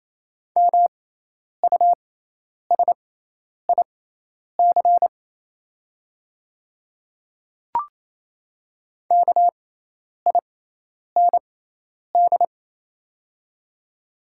Morse code, heard as a keyed tone, about 28 wpm, spaced as for 10 wpm, 700 Hz. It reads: MUSIC KIND
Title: 6x Character Spacing - Top 500 Words - 28wpm